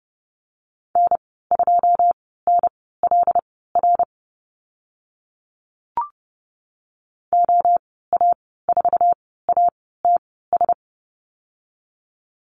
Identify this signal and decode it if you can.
Morse code, heard as a keyed tone, about 30 wpm, spaced as for 17 wpm, 700 Hz. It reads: N2NLR OA4ATS
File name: Lesson 15g - Number 5 - Call Signs 30wpm